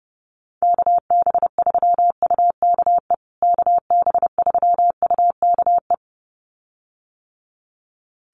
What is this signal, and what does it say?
Morse code, heard as a keyed tone, about 30 wpm, 700 Hz. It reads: KB3UKE KB3UKE
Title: USA Call Signs Repeated 2x 30wpm